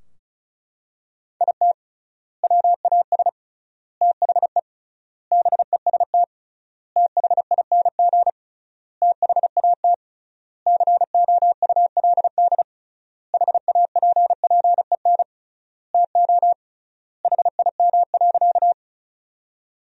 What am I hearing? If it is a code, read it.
Morse code, heard as a keyed tone, about 36 wpm, 700 Hz. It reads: IT WAS THE BEST THING THAT COULD HAPPEN TO HIM.
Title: 3x Word-Spacing - Sentences from Top 400 Words - 35wpm